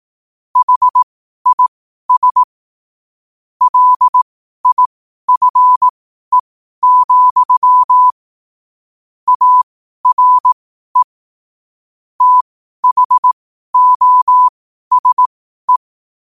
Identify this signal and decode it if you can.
Morse code, heard as a keyed tone, about 18 wpm, spaced as for 12 wpm, 1000 Hz. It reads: HIS LIFE, ARE THOSE
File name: qotdfiles / 12wpm